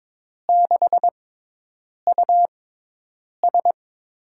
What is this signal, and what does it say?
Morse code, heard as a keyed tone, about 22 wpm, spaced as for 8 wpm, 700 Hz. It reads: 6US
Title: USA Call Signs - 5 Characters - 6x Character Spacing - 22wpm